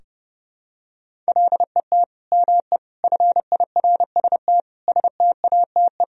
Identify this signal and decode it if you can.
Morse code, heard as a keyed tone, about 32 wpm, 700 Hz. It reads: LET ME FIRST STATE
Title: Sentences from Top 1000 Words - Review Part 2 - 30wpm